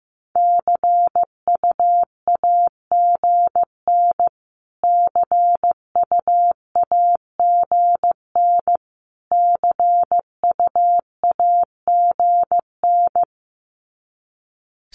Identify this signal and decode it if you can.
Morse code, heard as a keyed tone, about 15 wpm, 700 Hz. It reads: CUAGN CUAGN CUAGN
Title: Abbreviations Repeated 3x - 15wpm